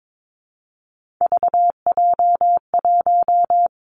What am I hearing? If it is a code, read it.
Morse code, heard as a keyed tone, about 22 wpm, 700 Hz. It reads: VJ1
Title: Three Challenging Letter-Number Combinations 22wpm